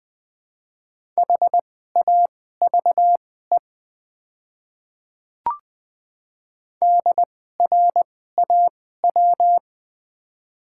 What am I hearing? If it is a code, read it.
Morse code, heard as a keyed tone, about 20 wpm, spaced as for 15 wpm, 700 Hz. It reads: HAVE DRAW
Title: 2x Character Spacing - Top 300 Words - 20wpm